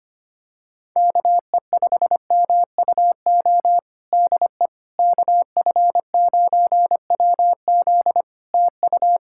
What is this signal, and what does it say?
Morse code, heard as a keyed tone, about 25 wpm, 700 Hz. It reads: KE5MUO DE KF9WZ TU
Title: K1USN SST QSO Practice 25wpm